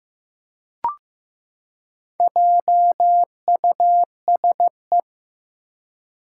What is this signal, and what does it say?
Morse code, heard as a keyed tone, about 15 wpm, 700 Hz. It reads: JUSE
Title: Four Challenging Letter-Number Combinations 15wpm